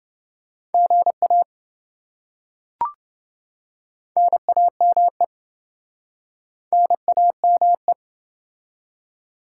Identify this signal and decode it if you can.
Morse code, heard as a keyed tone, about 26 wpm, 700 Hz. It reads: GA NAME NAME